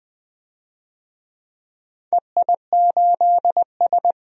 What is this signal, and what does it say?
Morse code, heard as a keyed tone, about 20 wpm, 700 Hz. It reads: EI8S
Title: DXCC-100 Call Signs - 4 Characters - 20wpm